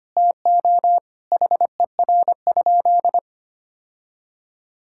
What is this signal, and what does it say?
Morse code, heard as a keyed tone, about 25 wpm, 700 Hz. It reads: TO HER?
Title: Sentences from Top 1000 Words - Review Part 6 - 25wpm